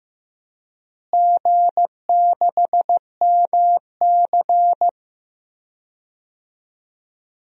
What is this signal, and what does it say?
Morse code, heard as a keyed tone, about 15 wpm, 700 Hz. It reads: G6MC
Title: DXCC-100 Call Signs - 4 Characters - Repeated 3x - 15wpm